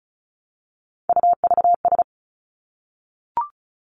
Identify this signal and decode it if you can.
Morse code, heard as a keyed tone, about 36 wpm, 700 Hz. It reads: UVS
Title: Three Challenging Letter-Number Combinations 35wpm